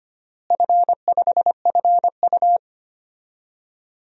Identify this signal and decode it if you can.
Morse code, heard as a keyed tone, about 25 wpm, 700 Hz. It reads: F5FU
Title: DXCC-100 Call Signs - 4 Characters - Repeated 3x - 25wpm